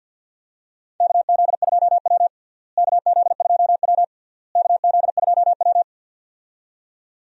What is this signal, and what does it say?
Morse code, heard as a keyed tone, about 53 wpm, 700 Hz. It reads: KZ2W KZ2W KZ2W
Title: USA Call Signs - 4 Characters - Repeated 3x - 50wpm